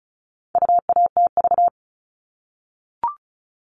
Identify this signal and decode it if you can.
Morse code, heard as a keyed tone, about 34 wpm, 700 Hz. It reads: UATV
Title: All Cut Numbers - 4-Digit 35wpm